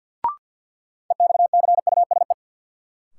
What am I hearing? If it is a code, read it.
Morse code, heard as a keyed tone, about 47 wpm, 700 Hz. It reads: EXCUSE
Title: Sentences from Top 700 Words - New Word Drill - 50wpm